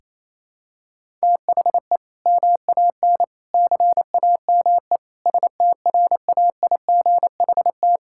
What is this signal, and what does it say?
Morse code, heard as a keyed tone, about 28 wpm, 690 Hz. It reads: THE MAN CAME STRAIGHT
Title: Sentences from Top 1000 Words - Review Part 6 - 28wpm